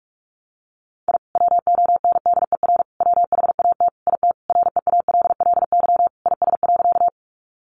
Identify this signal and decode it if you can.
Morse code, heard as a keyed tone, about 46 wpm, 700 Hz. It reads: I WONDER WHAT IT REALLY IS.
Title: Sentences from Top 600 Words 45wpm